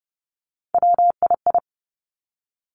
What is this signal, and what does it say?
Morse code, heard as a keyed tone, about 30 wpm, 700 Hz. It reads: WII
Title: Three Challenging Letter-Number Combinations 30wpm